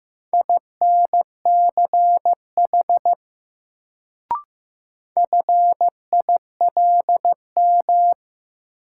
Morse code, heard as a keyed tone, about 15 wpm, 700 Hz. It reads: INCH FILM
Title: Sentences from Top 1000 Words - Review Part 3 - New Word Drill - 15wpm